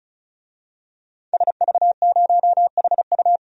Morse code, heard as a keyed tone, about 36 wpm, 700 Hz. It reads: SV0HU
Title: DXCC-100 Call Signs - 5 Characters - 35wpm